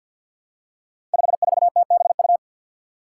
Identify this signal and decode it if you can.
Morse code, heard as a keyed tone, about 49 wpm, 700 Hz. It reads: 54TBU
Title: Five Challenging Letter-Number Combinations 50wpm